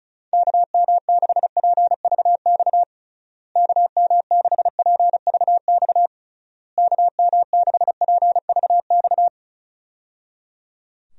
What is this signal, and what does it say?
Morse code, heard as a keyed tone, about 34 wpm, 700 Hz. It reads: KM6PVX KM6PVX KM6PVX
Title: USA Call Signs - 6 Characters - Repeated 3x - 35wpm